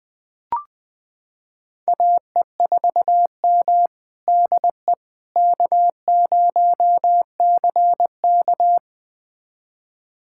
Morse code, heard as a keyed tone, about 20 wpm, 700 Hz. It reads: AE4M DE K0CK
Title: USA Callsign DE Callsign - 4 Characters - 20wpm